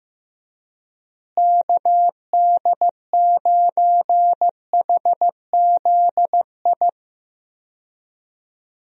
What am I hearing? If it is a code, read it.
Morse code, heard as a keyed tone, about 15 wpm, 700 Hz. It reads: KD9HZI
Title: USA Call Signs - 6 Characters - Repeated 2x - 15wpm